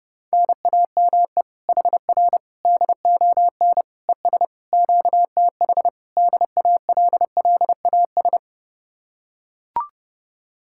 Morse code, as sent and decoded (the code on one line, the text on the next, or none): -. .- -- . / .... .-. / -.. --- -. / . ... / --.- - .... / -.. .- .-.. .-.. .- ...
NAME HR DON ES QTH DALLAS